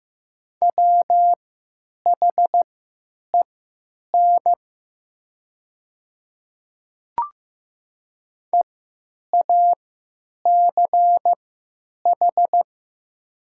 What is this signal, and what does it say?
Morse code, heard as a keyed tone, about 15 wpm, spaced as for 9 wpm, 700 Hz. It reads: WHEN EACH
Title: 3x Character Spacing - Top 200 Words - 15wpm